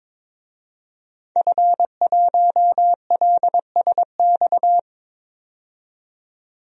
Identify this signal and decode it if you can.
Morse code, heard as a keyed tone, about 22 wpm, 700 Hz. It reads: F1LSX
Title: DXCC-100 Call Signs - 5 Characters - 22wpm